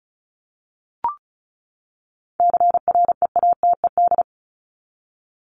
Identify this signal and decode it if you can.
Morse code, heard as a keyed tone, about 34 wpm, 700 Hz. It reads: CREATED
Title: ED Suffix - Top 100 Words with Base Words 35wpm